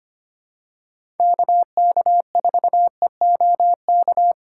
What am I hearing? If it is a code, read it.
Morse code, heard as a keyed tone, about 25 wpm, 700 Hz. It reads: KK4EOK
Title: USA Call Signs - 6 Characters - 4x Character Spacing - 25wpm